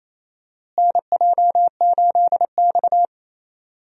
Morse code, heard as a keyed tone, about 28 wpm, 700 Hz. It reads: NJ8X
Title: USA Call Signs - 4 Characters - Repeated 2x - 28wpm